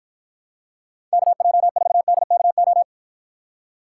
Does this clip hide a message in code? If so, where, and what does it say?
Yes: Morse code from 1.1 to 2.8 s, heard as a keyed tone, about 49 wpm, 700 Hz. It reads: KJ4DKO